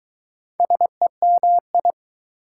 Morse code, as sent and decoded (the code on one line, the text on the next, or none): ... -- ..
SEMI